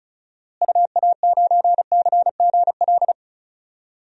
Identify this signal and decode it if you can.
Morse code, heard as a keyed tone, about 36 wpm, 700 Hz. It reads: UA9CGL
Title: DXCC-100 Call Signs - 6 Characters - Repeated 3x - 35wpm